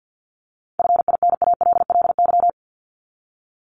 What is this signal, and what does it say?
Morse code, heard as a keyed tone, about 50 wpm, 700 Hz. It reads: FINALLY
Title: Top 751-1000 Words 50wpm